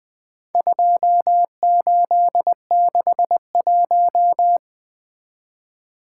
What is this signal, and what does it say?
Morse code, heard as a keyed tone, about 21 wpm, 700 Hz. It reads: 2861
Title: Four Letter-Number Combinations - 20wpm